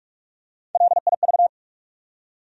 Morse code, heard as a keyed tone, about 45 wpm, 700 Hz. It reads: LIV
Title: USA Common Names - Max 3 Letters - 45wpm